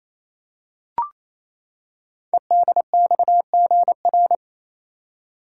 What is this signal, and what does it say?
Morse code, heard as a keyed tone, about 28 wpm, 700 Hz. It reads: EDXGR
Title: Code Groups - 5 Letters - 28wpm